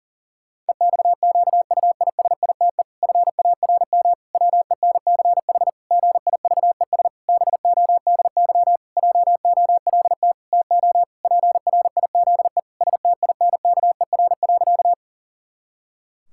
Morse code, heard as a keyed tone, about 40 wpm, 700 Hz. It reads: EXQUISITE FARM WENCH GIVES BODY JOLT TO PRIZE STINKER.
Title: Pangram Challenge v2 - 40wpm